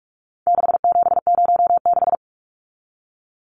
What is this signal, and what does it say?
Morse code, heard as a keyed tone, about 44 wpm, 700 Hz. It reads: 6706
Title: Numbers - 4-Digit 45wpm